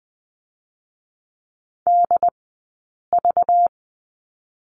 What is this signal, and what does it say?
Morse code, heard as a keyed tone, about 20 wpm, 700 Hz. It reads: D V
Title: 217 Lesson 35d - Number 0 - Two Characters 20wpm